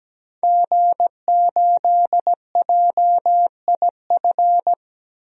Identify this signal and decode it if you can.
Morse code, heard as a keyed tone, about 17 wpm, 700 Hz. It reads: G8JIF
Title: DXCC-100 Call Signs - 5 Characters - 17wpm